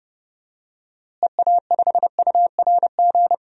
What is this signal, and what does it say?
Morse code, heard as a keyed tone, about 30 wpm, 700 Hz. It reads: EA5URG
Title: DXCC-100 Call Signs - 6 Characters - Repeated 3x - 30wpm